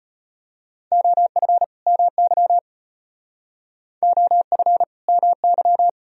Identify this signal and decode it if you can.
Morse code, heard as a keyed tone, about 31 wpm, 700 Hz. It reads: OF MY OF MY